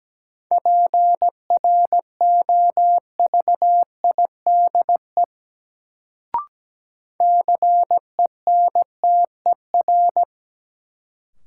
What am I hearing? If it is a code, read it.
Morse code, heard as a keyed tone, about 17 wpm, 700 Hz. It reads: PROVIDE CENTER
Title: Sentences from Top 1000 Words - Review Part 2 - New Word Drill - 17wpm